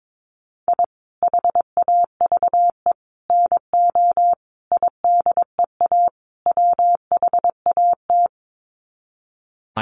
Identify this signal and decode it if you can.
Morse code, heard as a keyed tone, about 22 wpm, 700 Hz. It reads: I HAVE NO IDEA WHAT
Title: Sets of 5 Words Encore 22wpm